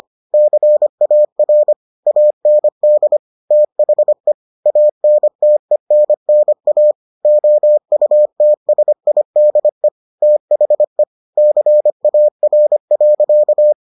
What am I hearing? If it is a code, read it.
Morse code, heard as a keyed tone, about 25 wpm, 600 Hz. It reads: CAR AND THE ANTENNA OUTSIDE THE CAR.